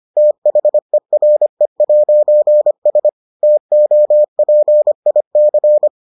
Morse code, heard as a keyed tone, about 25 wpm, 600 Hz. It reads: THERE'S TOPIC